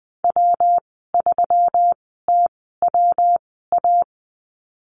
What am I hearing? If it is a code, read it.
Morse code, heard as a keyed tone, about 21 wpm, spaced as for 15 wpm, 700 Hz. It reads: W3TWA
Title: 122 Lesson 20g - Number 3 - Call Signs 20wpm